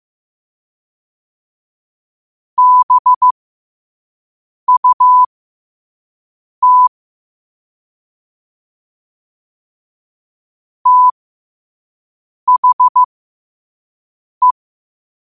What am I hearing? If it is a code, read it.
Morse code, heard as a keyed tone, about 15 wpm, spaced as for 5 wpm, 1000 Hz. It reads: BUT THE
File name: qotdfiles / 5wpm